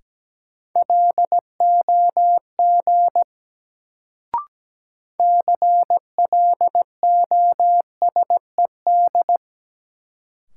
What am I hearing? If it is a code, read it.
Morse code, heard as a keyed tone, about 17 wpm, 700 Hz. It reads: LOG CLOSED
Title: Sentences from Top 1400 Words - New Word Drill - 17wpm